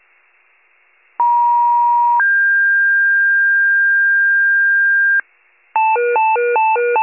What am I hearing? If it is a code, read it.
Two-tone page, heard as a one-second tone, then a three-second tone, about 950 then 1600 Hz, with noise. It is heard from 1.2 to 5.2 s.